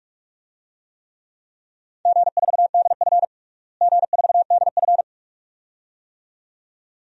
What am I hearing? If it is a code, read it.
Morse code, heard as a keyed tone, about 46 wpm, 700 Hz. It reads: G4DF G4DF
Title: DXCC-100 Call Signs - 4 Characters - Repeated 2x - 45wpm